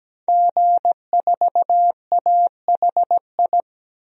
Morse code, heard as a keyed tone, about 17 wpm, 700 Hz. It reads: G4AHI
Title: DXCC-100 Call Signs - 5 Characters - 17wpm